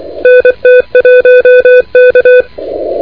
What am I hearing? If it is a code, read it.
Morse code, heard as a keyed tone, about 24 wpm, 2500 Hz, with noise. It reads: NT1K
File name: recordings / PR4C de NT1K ARRL10M CW 28MHz